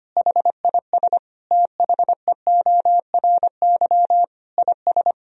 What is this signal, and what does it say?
Morse code, heard as a keyed tone, about 25 wpm, 700 Hz. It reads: HIS THEORY IS
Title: Sentences from Top 1000 Words - Review Part 3 - 25wpm